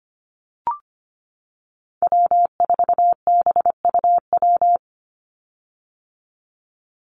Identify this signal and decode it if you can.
Morse code, heard as a keyed tone, about 25 wpm, 700 Hz. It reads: W4BUW
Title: Call Signs - 25wpm